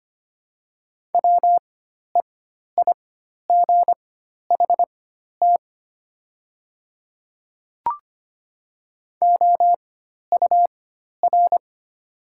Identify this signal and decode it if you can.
Morse code, heard as a keyed tone, about 25 wpm, spaced as for 12 wpm, 700 Hz. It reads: WEIGHT OUR
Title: 4x Character Spacing - Top 500 Words - 25wpm